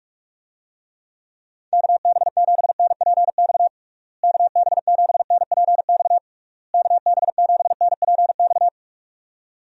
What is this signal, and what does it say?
Morse code, heard as a keyed tone, about 46 wpm, 700 Hz. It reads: KB7NPX KB7NPX KB7NPX
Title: USA Call Signs - 6 Characters - Repeated 3x - 45wpm